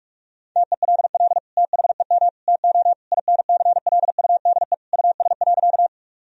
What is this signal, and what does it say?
Morse code, heard as a keyed tone, about 46 wpm, 700 Hz. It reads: TELL THEM TO INCLUDE US.
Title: Sentences from Top 1000 Words - Review Part 3 - 45wpm